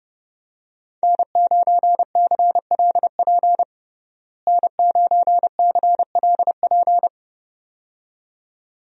Morse code, heard as a keyed tone, about 31 wpm, 700 Hz. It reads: N9CLP N9CLP
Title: USA Call Signs - 5 Characters - Repeated 2x - 30wpm